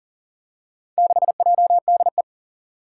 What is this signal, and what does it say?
Morse code, heard as a keyed tone, about 41 wpm, 700 Hz. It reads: 6JDE
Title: Four Challenging Letter-Number Combinations 40wpm